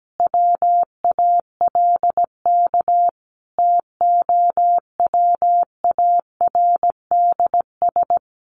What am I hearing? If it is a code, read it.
Morse code, heard as a keyed tone, about 17 wpm, 700 Hz. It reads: WALK TOWARDS
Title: Sentences from Top 1100 Words 17wpm